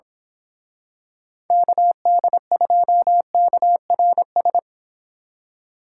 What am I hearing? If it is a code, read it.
Morse code, heard as a keyed tone, about 27 wpm, 700 Hz. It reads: KD2KRS
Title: Speed Racing - USA 2x3 Call Signs 22wpm